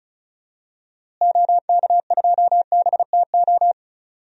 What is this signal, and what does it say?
Morse code, heard as a keyed tone, about 34 wpm, 700 Hz. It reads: OK2BTO
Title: DXCC-100 Call Signs - 6 Characters - Repeated 2x - 35wpm